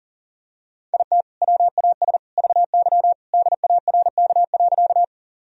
Morse code, heard as a keyed tone, about 40 wpm, 700 Hz. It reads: IT WAS VY DARK.
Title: Sentences from Top 500 Words and Abbreviations 40wpm